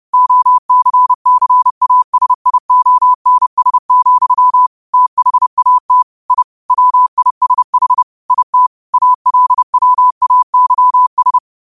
Morse code, heard as a keyed tone, about 30 wpm, 1000 Hz. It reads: OCCASIONS, THAT I WISH IT ALWAYS
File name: qotdfiles / 30wpm